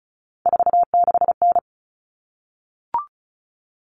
Morse code, heard as a keyed tone, about 34 wpm, 700 Hz. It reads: 46N
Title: All Cut Numbers - 3-Digit 35wpm